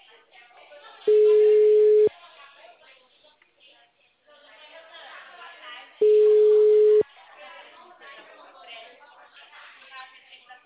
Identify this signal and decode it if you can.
Ringback tone (Europe, 425 Hz), with speech.